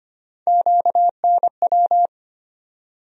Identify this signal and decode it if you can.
Morse code, heard as a keyed tone, about 25 wpm, 700 Hz. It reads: QNW
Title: USA Call Sign Postfixes - 3 Letters 25wpm